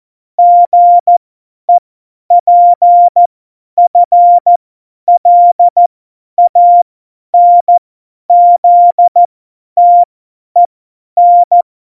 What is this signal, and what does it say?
Morse code, heard as a keyed tone, about 14 wpm, spaced as for 10 wpm, 700 Hz. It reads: GEPFLANZTEN
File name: Morsekurs / Witze / Witz 17. November 2020, 50 F70